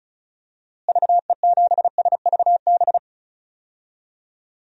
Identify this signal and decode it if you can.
Morse code, heard as a keyed tone, about 36 wpm, 700 Hz. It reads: VE7SVB
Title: DXCC-100 Call Signs - 6 Characters - Repeated 2x - 35wpm